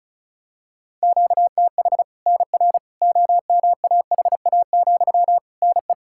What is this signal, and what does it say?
Morse code, heard as a keyed tone, about 34 wpm, 700 Hz. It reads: QTH NR OMAHA, NE
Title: Practice QSO - Exchange 1 - Beginning - 35wpm